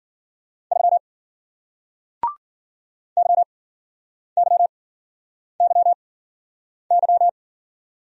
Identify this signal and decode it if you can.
Morse code, heard as a keyed tone, about 51 wpm, spaced as for 9 wpm, 700 Hz. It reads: 3 YYYY